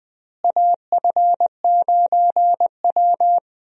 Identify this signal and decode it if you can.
Morse code, heard as a keyed tone, about 20 wpm, 700 Hz. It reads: AF9W
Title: USA Call Signs - 4 Characters - Repeated 2x - 20wpm